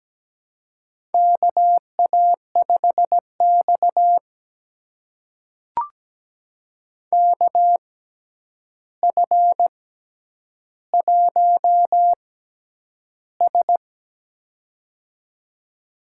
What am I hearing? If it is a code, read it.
Morse code, heard as a keyed tone, about 17 wpm, spaced as for 9 wpm, 700 Hz. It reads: KA5X KF1S
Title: USA Call Signs - 4 Characters - 6x Character Spacing - 17wpm